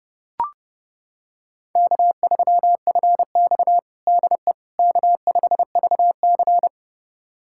Morse code, heard as a keyed tone, about 30 wpm, 700 Hz. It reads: K3FX DE K5VC
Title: USA Callsign DE Callsign - 4 Characters - 30wpm